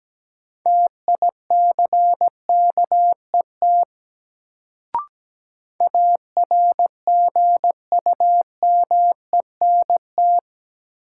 Morse code, heard as a keyed tone, about 17 wpm, 700 Hz. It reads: TICKET ARGUMENT
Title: Sentences from Top 1400 Words - New Word Drill - 17wpm